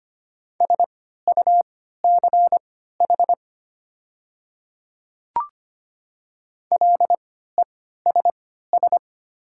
Morse code, heard as a keyed tone, about 25 wpm, spaced as for 14 wpm, 700 Hz. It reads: SUCH LESS
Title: 3x Character Spacing - Top 500 Words - 25wpm